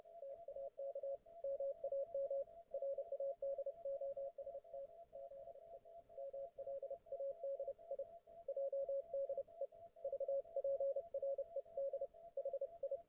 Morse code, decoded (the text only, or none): TAK MAM RADOST ZE MLADI JDE VPRED HI